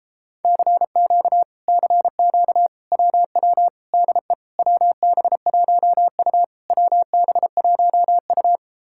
Morse code, heard as a keyed tone, about 38 wpm, 700 Hz. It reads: CQ CQ WW DE WB1U WB1U